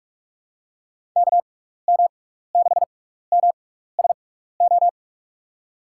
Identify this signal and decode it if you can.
Morse code, heard as a keyed tone, about 43 wpm, spaced as for 15 wpm, 700 Hz. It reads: KM6MSO
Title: USA Call Signs - 6 Characters - 6x Character Spacing - 45wpm